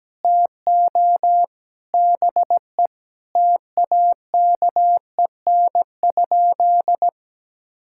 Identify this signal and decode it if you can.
Morse code, heard as a keyed tone, about 17 wpm, 700 Hz. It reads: TO BE TAKEN?
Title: Sentences from Top 1000 Words - Review Part 6 - 17wpm